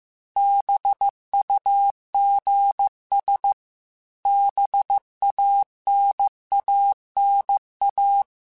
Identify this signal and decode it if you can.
Morse code, heard as a keyed tone, about 15 wpm, 800 Hz. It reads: BUGS BANANA